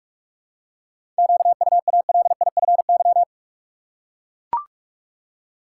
Morse code, heard as a keyed tone, about 42 wpm, 700 Hz. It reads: QUALIFY
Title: Seven Letter Words - 45wpm